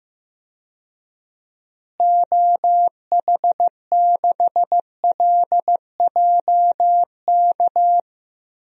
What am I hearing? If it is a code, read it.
Morse code, heard as a keyed tone, about 15 wpm, 700 Hz. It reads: OH6LJK
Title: DXCC-100 Call Signs - 6 Characters - Repeated 2x - 15wpm